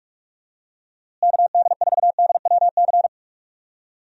Morse code, heard as a keyed tone, about 46 wpm, 700 Hz. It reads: KD4DWC
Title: USA Call Signs - 6 Characters - Repeated 3x - 45wpm